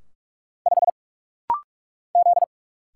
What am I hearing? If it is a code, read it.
Morse code, heard as a keyed tone, about 44 wpm, 700 Hz. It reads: 5 Z
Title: Single Letter-Number - Rapid-Fire 45wpm